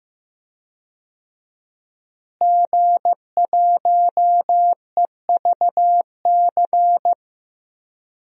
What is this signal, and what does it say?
Morse code, heard as a keyed tone, about 15 wpm, 700 Hz. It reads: G1EVC
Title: DXCC-100 Call Signs - 5 Characters - Repeated 3x - 15wpm